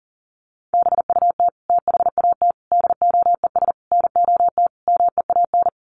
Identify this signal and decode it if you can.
Morse code, heard as a keyed tone, about 40 wpm, 700 Hz. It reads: BUT THAT DOES NOT MEAN
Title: Sets of 5 Words 40wpm